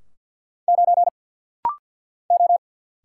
Morse code, heard as a keyed tone, about 50 wpm, 700 Hz. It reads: 9 O
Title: Single Letter-Number Alt - Rapid-Fire 50wpm